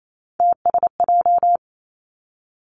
Morse code, heard as a keyed tone, about 28 wpm, 700 Hz. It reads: TSJ
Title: Three Challenging Letter-Number Combinations 28wpm